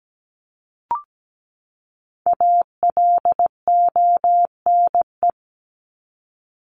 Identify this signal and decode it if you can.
Morse code, heard as a keyed tone, about 17 wpm, 700 Hz. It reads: ALONE
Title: Top 751-1000 Words 17wpm